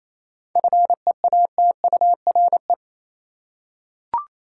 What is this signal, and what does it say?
Morse code, heard as a keyed tone, about 28 wpm, 700 Hz. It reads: FEATURE
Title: Seven Letter Words - 28wpm